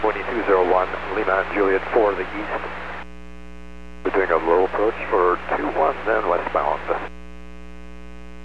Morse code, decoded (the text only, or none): none